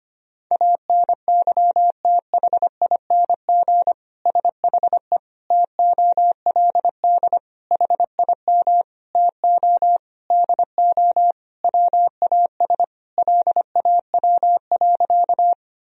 Morse code, heard as a keyed tone, about 25 wpm, 700 Hz. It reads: ANYTHING SHE TOLD HIM TO DO WAS LAW.
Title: Sentences from Top 1000 Words - Review Part 3 - 25wpm